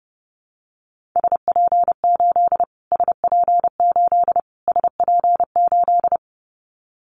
Morse code, heard as a keyed tone, about 30 wpm, 700 Hz. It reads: SP8 SP8 SP8
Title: Callsign Prefixes for Top 25 DXCC - Repeated 3x - 30wpm